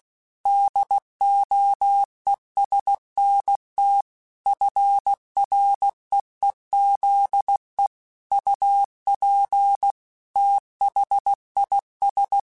Morse code, heard as a keyed tone, about 16 wpm, 800 Hz. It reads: DOESNT FREEZE UP THIS
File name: QSOs / QSOs-mp3s / WD0BGP 2@15